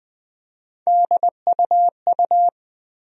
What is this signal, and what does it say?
Morse code, heard as a keyed tone, about 20 wpm, 700 Hz. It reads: DUU
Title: USA Call Sign Postfixes - 3 Letters 20wpm